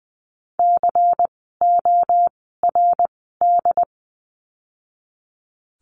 Morse code, heard as a keyed tone, about 20 wpm, spaced as for 14 wpm, 700 Hz. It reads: CORD
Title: 094 Lesson 16f - Letter c - Words 20wpm